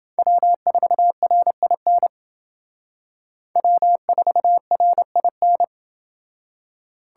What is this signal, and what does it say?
Morse code, heard as a keyed tone, about 29 wpm, 700 Hz. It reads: W4RIN W4RIN